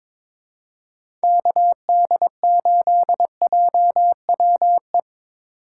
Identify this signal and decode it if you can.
Morse code, heard as a keyed tone, about 21 wpm, 700 Hz. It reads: KD8JWE
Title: USA Call Signs - 6 Characters - 2x Character Spacing - 22wpm